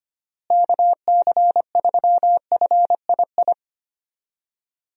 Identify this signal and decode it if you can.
Morse code, heard as a keyed tone, about 25 wpm, 700 Hz. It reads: KC3FII
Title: USA Call Signs - 6 Characters - Repeated 2x - 25wpm